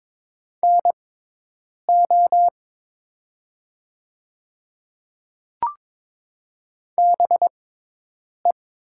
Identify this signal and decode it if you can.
Morse code, heard as a keyed tone, about 22 wpm, spaced as for 8 wpm, 700 Hz. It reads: NO BE
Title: 6x Character Spacing - Top 200 Words - 22wpm